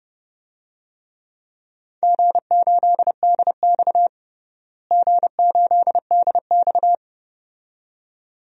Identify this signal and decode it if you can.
Morse code, heard as a keyed tone, about 30 wpm, 700 Hz. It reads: G8DX G8DX
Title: DXCC-100 Call Signs - 4 Characters - Repeated 2x - 30wpm